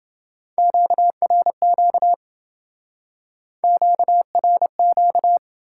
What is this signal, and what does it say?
Morse code, heard as a keyed tone, about 28 wpm, 700 Hz. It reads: QRQ QRQ